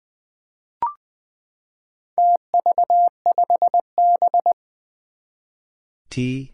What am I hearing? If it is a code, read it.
Morse code, heard as a keyed tone, about 20 wpm, 700 Hz. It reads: TV5B
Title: Four Challenging Letter-Number Combinations 20wpm